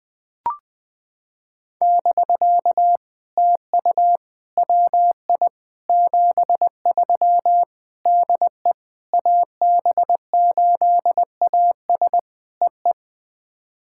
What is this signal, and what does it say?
Morse code, heard as a keyed tone, about 20 wpm, 700 Hz. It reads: <BK> TU WI 73 DE AB8AS EE
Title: POTA QSO Practice - 20wpm